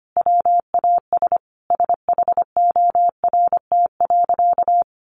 Morse code, heard as a keyed tone, about 25 wpm, 700 Hz. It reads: WAS SHORT.
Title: Sentences from Top 1300 Words 25wpm